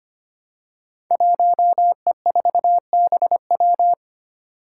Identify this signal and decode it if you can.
Morse code, heard as a keyed tone, about 25 wpm, 700 Hz. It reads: 1E4BW
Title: Five Challenging Letter-Number Combinations 25wpm